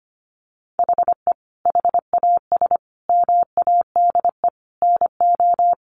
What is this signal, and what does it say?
Morse code, heard as a keyed tone, about 25 wpm, 700 Hz. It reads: HE HAS MADE NO
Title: Sentences from Top 1200 Words 25wpm